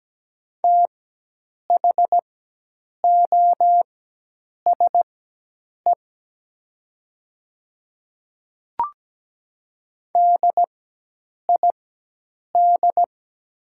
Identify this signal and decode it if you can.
Morse code, heard as a keyed tone, about 17 wpm, spaced as for 8 wpm, 700 Hz. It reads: THOSE DID
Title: 4x Character Spacing - Top 300 Words - 17wpm